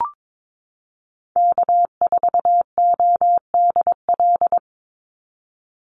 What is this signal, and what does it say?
Morse code, heard as a keyed tone, about 22 wpm, 700 Hz. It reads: K4ODL
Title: Call Signs Repeated 2x - 22wpm